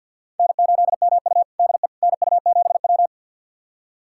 Morse code, heard as a keyed tone, about 47 wpm, 700 Hz. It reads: N8MU DE NU7W
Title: USA Callsign DE Callsign - 4 Characters - 50wpm